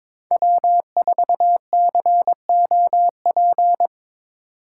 Morse code, heard as a keyed tone, about 23 wpm, 700 Hz. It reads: W4COP
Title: USA Call Signs - 8x Character Spacing - 22wpm